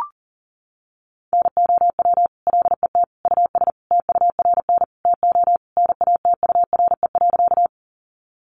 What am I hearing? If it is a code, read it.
Morse code, heard as a keyed tone, about 40 wpm, 700 Hz. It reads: NOW LET US TURN TO NATURE.
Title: Sentences from Top 600 Words 40wpm